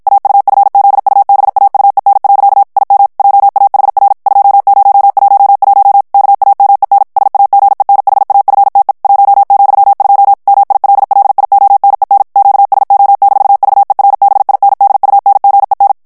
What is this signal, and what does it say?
Morse code, heard as a keyed tone, about 52 wpm, 800 Hz. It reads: AUFZUBAUEN. IM JAHR 2011 KAMEN SAGENHAFTE 1,2 MILLIONEN QSO-VERBINDUNGEN
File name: morse mp3 / ALLEMAND texte clair 2 vitesse 50 mots minute